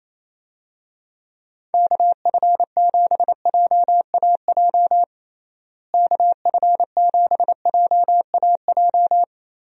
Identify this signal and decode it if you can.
Morse code, heard as a keyed tone, about 28 wpm, 700 Hz. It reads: KF7JAJ KF7JAJ
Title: USA Call Signs - 6 Characters - Repeated 2x - 28wpm